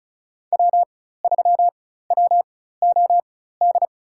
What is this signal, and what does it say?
Morse code, heard as a keyed tone, about 34 wpm, spaced as for 16 wpm, 700 Hz. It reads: W3WOD
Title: USA Call Signs - 5 Characters - 4x Character Spacing - 35wpm